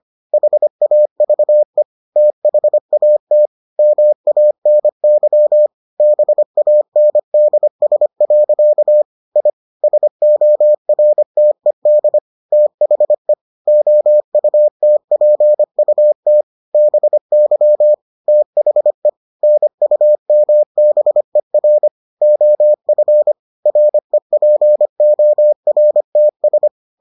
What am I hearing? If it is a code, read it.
Morse code, heard as a keyed tone, about 25 wpm, 600 Hz. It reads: HAVE THAT MANY BANDS. I SORTED THE OUTPUT BY THE NUMBER OF REPORTS